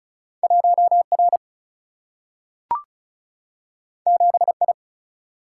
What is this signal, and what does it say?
Morse code, heard as a keyed tone, about 35 wpm, 700 Hz. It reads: 1R 7I
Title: Single-Letter Single-Number 35wpm